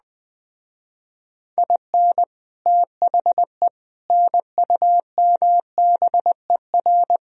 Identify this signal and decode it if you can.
Morse code, heard as a keyed tone, about 20 wpm, 700 Hz. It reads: IN THE NUMBER
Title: Speed-Racing - Sets of 4 Words 17wpm